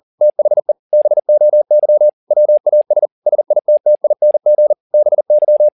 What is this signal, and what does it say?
Morse code, heard as a keyed tone, about 40 wpm, 600 Hz. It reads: THE BOY WAS SITTING BY
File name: the old man and the sea 2 40wpm